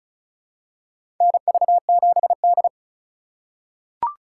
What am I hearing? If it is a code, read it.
Morse code, heard as a keyed tone, about 34 wpm, 700 Hz. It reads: NV7D
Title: USA Call Signs - 4 Characters - 4x Character Spacing - 35wpm